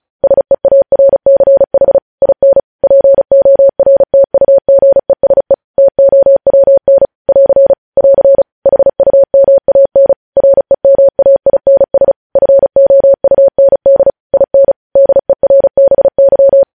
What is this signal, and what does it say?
Morse code, heard as a keyed tone, about 36 wpm, 550 Hz. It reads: SEARCH IN PORTUGESE TOWN <AR> <AR> HUMAN REMAINS FOUND IN DERBY
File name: podcast / News headlines-1x-35x35